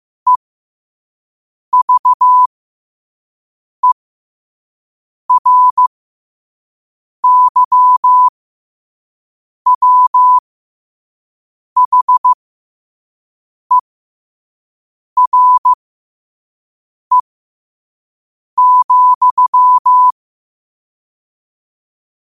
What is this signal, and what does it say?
Morse code, heard as a keyed tone, about 15 wpm, spaced as for 5 wpm, 1000 Hz. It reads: EVERYWHERE,
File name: qotdfiles / 5wpm